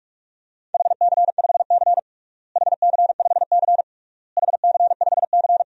Morse code, heard as a keyed tone, about 46 wpm, 700 Hz. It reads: HC5C HC5C HC5C
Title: DXCC-100 Call Signs - 4 Characters - Repeated 3x - 45wpm